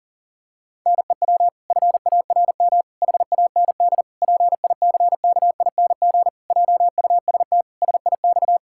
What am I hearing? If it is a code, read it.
Morse code, heard as a keyed tone, about 40 wpm, 700 Hz. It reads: NEW FARM HAND PICKING JUST SIX
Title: Pangram Challenge v2 - 40wpm